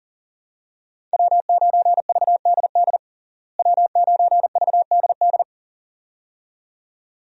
Morse code, heard as a keyed tone, about 40 wpm, 700 Hz. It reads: W9VDD W9VDD
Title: USA Call Signs - 5 Characters - Repeated 2x - 40wpm